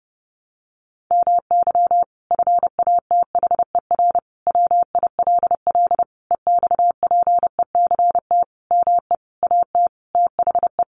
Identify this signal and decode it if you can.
Morse code, heard as a keyed tone, about 30 wpm, 700 Hz. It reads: MY FATHER WILL EXPECT ME AT THE
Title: Sentences from Top 1100 Words 30wpm